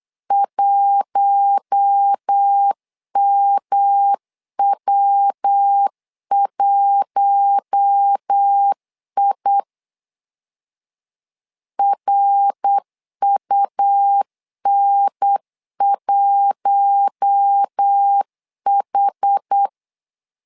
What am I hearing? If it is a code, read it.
Morse code, heard as a keyed tone, about 8 wpm, 800 Hz. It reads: 1MW1I RUN1H
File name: LEZIONI / 4 Q1 8wpm 3035